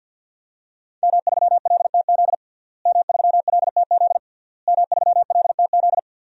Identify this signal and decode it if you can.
Morse code, heard as a keyed tone, about 46 wpm, 700 Hz. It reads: M3LTZ M3LTZ M3LTZ
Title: DXCC-100 Call Signs - 5 Characters - Repeated 3x - 50wpm